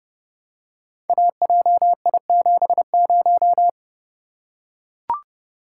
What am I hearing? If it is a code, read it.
Morse code, heard as a keyed tone, about 30 wpm, 700 Hz. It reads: AJI70
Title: Code Groups - 5 Letter:Numbers - 30wpm